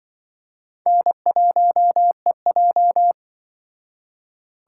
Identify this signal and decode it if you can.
Morse code, heard as a keyed tone, about 24 wpm, 700 Hz. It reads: N1EJ
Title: Speed Racing - USA 1x2 Call Signs 20wpm